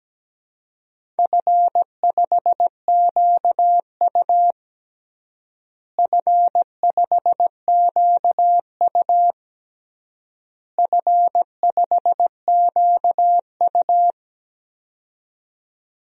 Morse code, heard as a keyed tone, about 17 wpm, 700 Hz. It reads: F5QU F5QU F5QU
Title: DXCC-100 Call Signs - 4 Characters - Repeated 3x - 17wpm